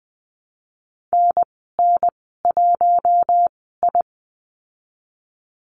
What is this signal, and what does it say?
Morse code, heard as a keyed tone, about 20 wpm, spaced as for 15 wpm, 700 Hz. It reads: NN1I
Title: 048 Lesson 9g - Number 4 - Call Signs 20wpm